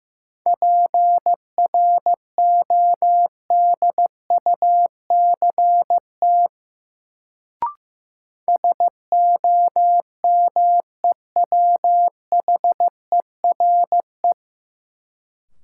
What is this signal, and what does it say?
Morse code, heard as a keyed tone, about 15 wpm, 700 Hz. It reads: PRODUCT SOMEWHERE
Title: Sentences from Top 1400 Words - New Word Drill Alt - 15wpm